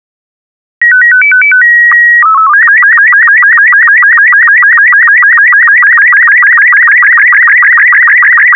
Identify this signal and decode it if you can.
SSTV (slow-scan television), Robot 36.